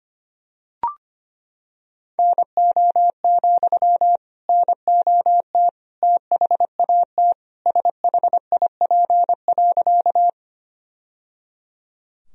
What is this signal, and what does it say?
Morse code, heard as a keyed tone, about 25 wpm, 700 Hz. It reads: NO, NOT THAT SHIP.
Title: Sentences from Top 1000 Words - Review Part 2 - 25wpm